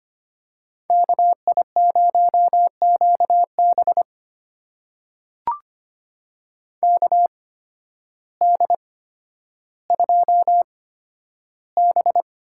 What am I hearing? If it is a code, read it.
Morse code, heard as a keyed tone, about 25 wpm, spaced as for 12 wpm, 700 Hz. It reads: KI0QB KD2B
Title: USA Call Signs - 8x Character Spacing - 25wpm